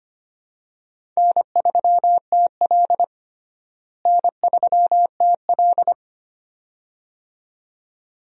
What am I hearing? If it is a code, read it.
Morse code, heard as a keyed tone, about 25 wpm, 700 Hz. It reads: N3TL N3TL
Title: USA Call Signs - 4 Characters - Repeated 2x - 25wpm